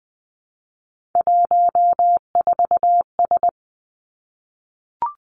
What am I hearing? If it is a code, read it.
Morse code, heard as a keyed tone, about 19 wpm, 700 Hz. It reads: 14S
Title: Three Challenging Letter-Number Combinations 20wpm